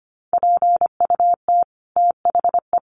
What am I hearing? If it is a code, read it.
Morse code, heard as a keyed tone, about 25 wpm, 700 Hz. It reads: PUT THE
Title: Sentences from Top 1400 Words 25wpm